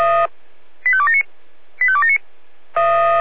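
Five-tone selective call (ZVEI), heard as quick runs of five tones, tones of about 70 ms. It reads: 85279 85279
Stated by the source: ZVEI five-tone call 85211 received by SDR.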